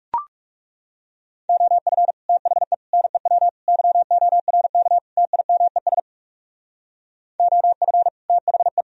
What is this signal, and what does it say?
Morse code, heard as a keyed tone, about 39 wpm, 700 Hz. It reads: OF THE NEW YORK TIMES OF THE